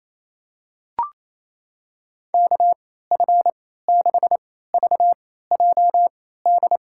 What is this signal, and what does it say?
Morse code, heard as a keyed tone, about 27 wpm, spaced as for 16 wpm, 700 Hz. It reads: KF6VJD
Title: USA Call Signs - 6 Characters - 3x Character Spacing - 28wpm